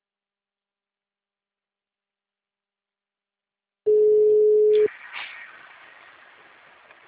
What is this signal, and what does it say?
Ringback tone (Europe, 425 Hz).
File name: out- -2021- - - .36